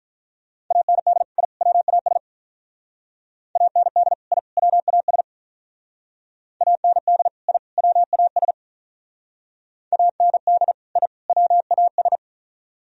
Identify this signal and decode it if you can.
Morse code, heard as a keyed tone, about 42 wpm, 700 Hz. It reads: AND I WAS AND I WAS AND I WAS AND I WAS
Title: Speed-Racing - Sets of 3 Words 35wpm